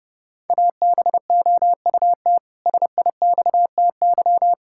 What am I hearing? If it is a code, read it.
Morse code, heard as a keyed tone, about 32 wpm, 700 Hz. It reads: ABOUT SIXTY
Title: Pangram Challenge v2 - 30wpm